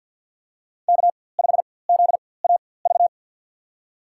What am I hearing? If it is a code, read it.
Morse code, heard as a keyed tone, about 53 wpm, spaced as for 24 wpm, 700 Hz. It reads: K5ZAV
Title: USA Call Signs - 5 Characters - 4x Character Spacing - 50wpm